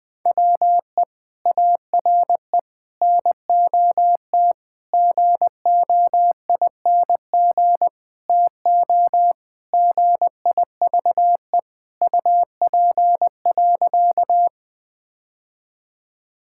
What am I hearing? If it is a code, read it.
Morse code, heard as a keyed tone, about 21 wpm, 700 Hz. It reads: WE ARE NOT GOING TO GIVE UP.
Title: Sentences with 5-Word Phrases - 20wpm